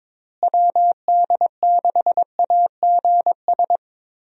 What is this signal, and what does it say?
Morse code, heard as a keyed tone, about 22 wpm, 700 Hz. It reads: WD6AGS
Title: USA Call Signs - 6 Characters - Repeated 3x - 22wpm